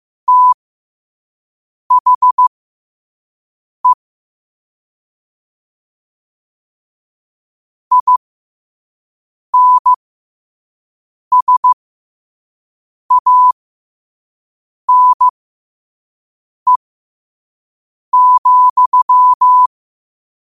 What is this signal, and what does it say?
Morse code, heard as a keyed tone, about 15 wpm, spaced as for 5 wpm, 1000 Hz. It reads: THE INSANE,
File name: qotdfiles / 5wpm